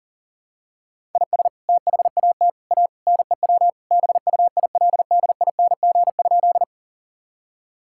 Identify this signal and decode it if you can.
Morse code, heard as a keyed tone, about 37 wpm, 700 Hz. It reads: IS THAT A NEW BUILDING?
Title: Sentences from Top 1000 Words - Review Part 5 - 40wpm